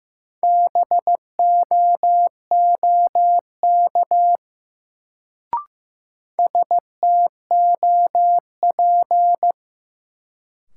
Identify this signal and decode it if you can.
Morse code, heard as a keyed tone, about 15 wpm, 700 Hz. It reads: BOOK STOP
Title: Sentences from Top 300 Words - New Word Drill - 15wpm